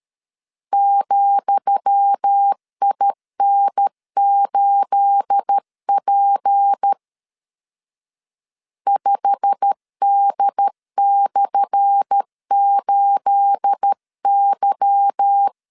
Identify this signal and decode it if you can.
Morse code, heard as a keyed tone, about 13 wpm, 800 Hz. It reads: ,IN8P 5D/8Y
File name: LEZIONI / 10 SIGNS / 10D COMA 12wpm 2544